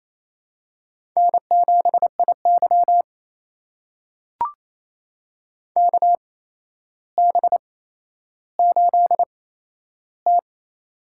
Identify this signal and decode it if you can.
Morse code, heard as a keyed tone, about 28 wpm, spaced as for 12 wpm, 700 Hz. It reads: N7IY KB8T
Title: USA Call Signs - 4 Characters - 8x Character Spacing - 28wpm